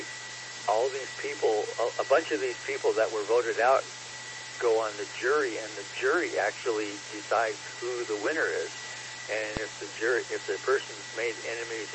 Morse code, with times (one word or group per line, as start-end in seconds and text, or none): none